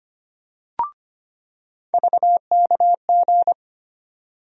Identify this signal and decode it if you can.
Morse code, heard as a keyed tone, about 25 wpm, 700 Hz. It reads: VKG
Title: USA Call Sign Postfixes - 3 Letters 25wpm